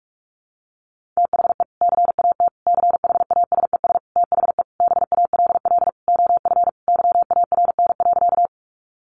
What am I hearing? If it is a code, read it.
Morse code, heard as a keyed tone, about 44 wpm, 700 Hz. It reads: THE CAT CHASES THE BALL OF YARN.
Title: Additional Easy Phrases 45wpm